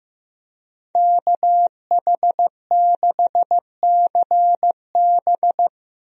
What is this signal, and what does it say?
Morse code, heard as a keyed tone, about 16 wpm, 700 Hz. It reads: KH6CB
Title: USA Call Signs - 5 Characters - Repeated 2x - 15wpm